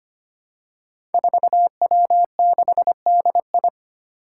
Morse code, heard as a keyed tone, about 25 wpm, 700 Hz. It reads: 4W6DI
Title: Five Challenging Letter-Number Combinations 25wpm